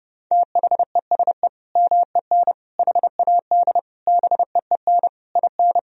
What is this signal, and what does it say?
Morse code, heard as a keyed tone, about 28 wpm, 700 Hz. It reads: THESE MEN HAD BEEN IN
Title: Sentences from Top 1000 Words - Review Part 1 - 30wpm